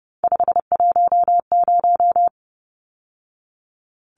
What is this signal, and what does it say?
Morse code, heard as a keyed tone, about 30 wpm, 700 Hz. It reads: 510
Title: Numbers - 3-Digit 30wpm